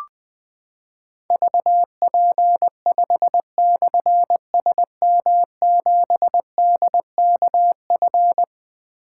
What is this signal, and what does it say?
Morse code, heard as a keyed tone, about 20 wpm, 700 Hz. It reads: VP5/SM7DKF
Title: Difficult Call Signs 20wpm